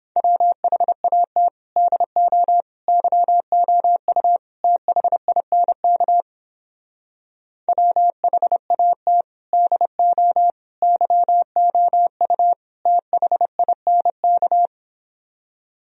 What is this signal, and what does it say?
Morse code, heard as a keyed tone, about 29 wpm, 700 Hz. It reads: WHAT DO YOU THINK WHAT DO YOU THINK